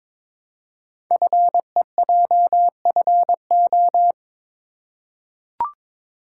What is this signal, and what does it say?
Morse code, heard as a keyed tone, about 23 wpm, 700 Hz. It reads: FEJFO
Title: Code Groups - 5 Letters - 22wpm